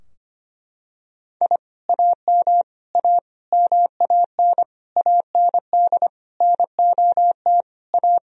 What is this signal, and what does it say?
Morse code, heard as a keyed tone, about 25 wpm, 700 Hz. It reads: I AM A MAN AND NOT A
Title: Sentences from Top 1000 Words - Review Part 2 - 25wpm